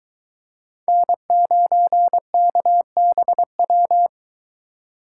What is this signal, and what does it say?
Morse code, heard as a keyed tone, about 22 wpm, 700 Hz. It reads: N9KBW